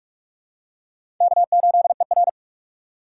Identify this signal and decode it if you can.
Morse code, heard as a keyed tone, about 43 wpm, 700 Hz. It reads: K8ER
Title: USA Call Signs - 4 Characters - 4x Character Spacing - 45wpm